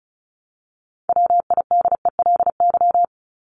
Morse code, heard as a keyed tone, about 34 wpm, 700 Hz. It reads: WIDELY
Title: LY Suffix - Top 100 Words with Base Words 35wpm